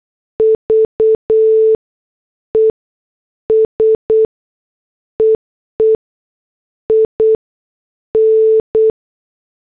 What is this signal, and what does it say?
Morse code, heard as a keyed tone, about 8 wpm, spaced as for 6 wpm, 440 Hz. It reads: VES EEIN
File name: morse / headlines